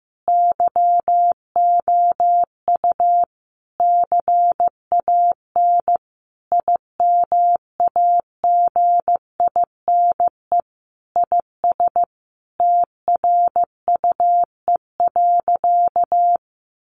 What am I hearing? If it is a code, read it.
Morse code, heard as a keyed tone, about 15 wpm, 700 Hz. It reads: YOU CAN IMAGINE IS TRUE.
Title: Sentences from Top 1400 Words 15wpm